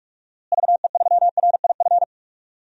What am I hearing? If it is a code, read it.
Morse code, heard as a keyed tone, about 46 wpm, 700 Hz. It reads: VE3RIF
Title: DXCC-100 Call Signs - 6 Characters - Repeated 3x - 45wpm